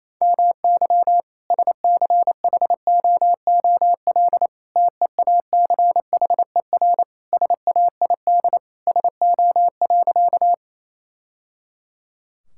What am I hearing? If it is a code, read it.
Morse code, heard as a keyed tone, about 28 wpm, 700 Hz. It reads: MY SCHOOL TEACHER SAID SO.
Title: Sentences from Top 1000 Words - Review Part 6 - 28wpm